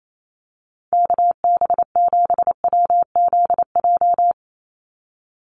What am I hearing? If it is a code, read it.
Morse code, heard as a keyed tone, about 28 wpm, 700 Hz. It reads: KB7WZJ
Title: USA Call Signs 28wpm